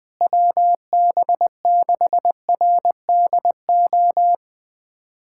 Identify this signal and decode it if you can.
Morse code, heard as a keyed tone, about 20 wpm, 700 Hz. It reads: WB6RDO
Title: USA Callsign DE Callsign 20wpm